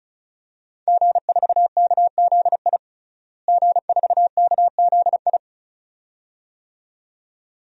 Morse code, heard as a keyed tone, about 36 wpm, 700 Hz. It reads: G4KZI G4KZI
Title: DXCC-100 Call Signs - 5 Characters - Repeated 2x - 35wpm